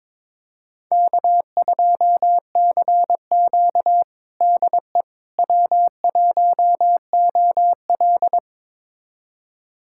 Morse code, heard as a keyed tone, about 22 wpm, 700 Hz. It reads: K2CQ DE W1OL
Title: USA Callsign DE Callsign - 4 Characters - 22wpm